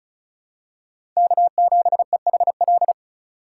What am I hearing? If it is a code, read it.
Morse code, heard as a keyed tone, about 36 wpm, 700 Hz. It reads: K7EHL
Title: USA Call Signs - 8x Character Spacing - 35wpm